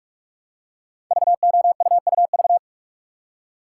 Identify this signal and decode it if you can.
Morse code, heard as a keyed tone, about 44 wpm, 700 Hz. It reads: VOUUV
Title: Code Groups - 5 Letters - 45wpm